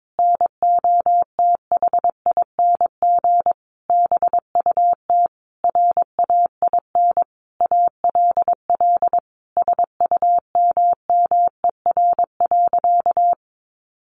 Morse code, heard as a keyed tone, about 22 wpm, 700 Hz. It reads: NOTHING BUT RAIN ALL SUMMER.
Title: Sentences from Top 1100 Words 22wpm